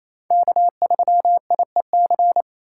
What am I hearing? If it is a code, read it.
Morse code, heard as a keyed tone, about 33 wpm, 700 Hz. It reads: K3IEC
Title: Speed Racing - USA 1x3 Call Signs 28wpm